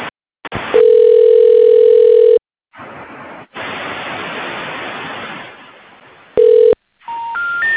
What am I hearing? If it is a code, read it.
Ringback tone (North America), 440 + 480 Hz.